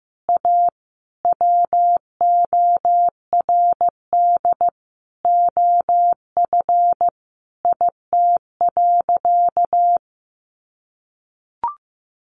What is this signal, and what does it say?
Morse code, heard as a keyed tone, about 15 wpm, 700 Hz. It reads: A WORD OF IT.
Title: Sentences from Top 1200 Words 15wpm